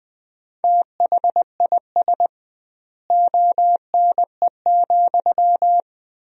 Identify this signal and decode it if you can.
Morse code, heard as a keyed tone, about 20 wpm, 700 Hz. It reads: THIS ONE,
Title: 2x Word-Spacing - Sentences from Top 200 Words - 20wpm